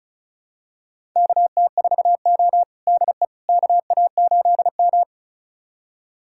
Morse code, heard as a keyed tone, about 34 wpm, 700 Hz. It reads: KT4O DE KA8M
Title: USA Callsign DE Callsign - 4 Characters - 35wpm